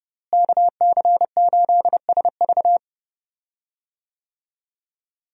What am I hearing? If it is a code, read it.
Morse code, heard as a keyed tone, about 32 wpm, 700 Hz. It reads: KC8SV
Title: USA Call Signs - 5 Characters - Repeated 3x - 30wpm